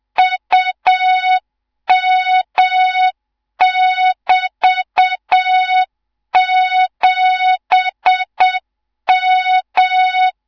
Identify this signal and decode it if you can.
Morse code, heard as a keyed tone, about 7 wpm, 3700 Hz, with noise. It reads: UM<BT>7M
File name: Morze-HA2MN / hang / 18 IRASJELEK2